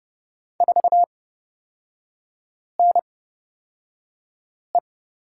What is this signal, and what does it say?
Morse code, heard as a keyed tone, about 30 wpm, spaced as for 5 wpm, 700 Hz. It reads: 4NE
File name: Lesson 37e - Letter x - Three Characters 30wpm